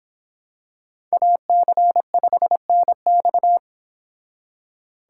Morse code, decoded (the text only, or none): AC5NX